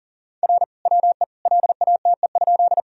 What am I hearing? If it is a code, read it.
Morse code, heard as a keyed tone, about 40 wpm, 700 Hz. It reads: R WE LATE?
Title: Sentences from Top 300 Words and Abbreviations 40wpm